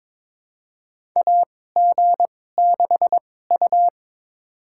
Morse code, heard as a keyed tone, about 22 wpm, spaced as for 16 wpm, 700 Hz. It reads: AG6U
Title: USA Call Signs - 4 Characters - 2x Character Spacing - 22wpm